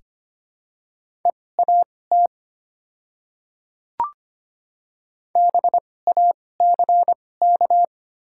Morse code, heard as a keyed tone, about 25 wpm, spaced as for 18 wpm, 700 Hz. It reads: EAT BACK
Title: 2x Character Spacing - Top 300 Words - 25wpm